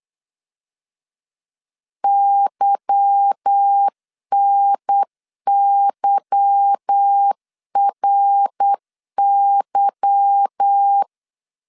Morse code, heard as a keyed tone, about 8 wpm, 800 Hz. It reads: YNYRY
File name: LEZIONI / 8 YJ45 / 8 Y 8wpm 3245